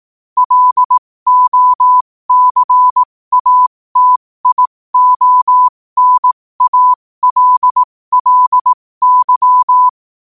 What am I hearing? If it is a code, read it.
Morse code, heard as a keyed tone, about 18 wpm, spaced as for 15 wpm, 1000 Hz. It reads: LOCATIONALLY
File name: qotdfiles / 15wpm